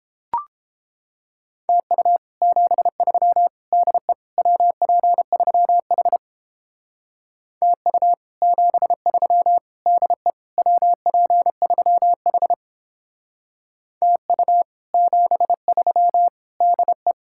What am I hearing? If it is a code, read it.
Morse code, heard as a keyed tone, about 29 wpm, 700 Hz. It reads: TU 73 DE WP3H TU 73 DE WP3H TU 73 DE